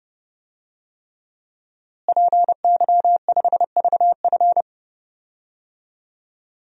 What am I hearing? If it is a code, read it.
Morse code, heard as a keyed tone, about 32 wpm, 700 Hz. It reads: PY5VF
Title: DXCC-100 Call Signs - 5 Characters - Repeated 3x - 30wpm